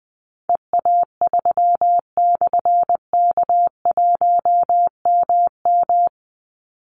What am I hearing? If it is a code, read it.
Morse code, heard as a keyed tone, about 19 wpm, 700 Hz. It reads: EA3/K1MM
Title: Call Signs - 20wpm